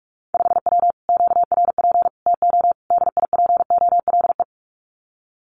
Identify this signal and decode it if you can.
Morse code, heard as a keyed tone, about 44 wpm, 700 Hz. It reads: 5W QRP TO DIPOLE
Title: Common QSO Phrases Encore 45wpm